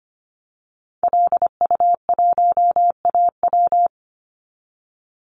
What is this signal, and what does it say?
Morse code, heard as a keyed tone, about 25 wpm, 700 Hz. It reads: LU1AW
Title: Call Signs - 25wpm